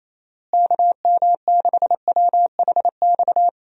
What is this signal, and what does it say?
Morse code, heard as a keyed tone, about 28 wpm, 700 Hz. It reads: KM6WHX
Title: USA Call Signs - 6 Characters - Repeated 3x - 28wpm